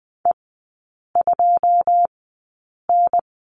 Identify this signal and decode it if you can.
Morse code, heard as a keyed tone, about 20 wpm, spaced as for 8 wpm, 700 Hz. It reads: E2N
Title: 114 Lesson 19e - Letter w - Three Characters 20wpm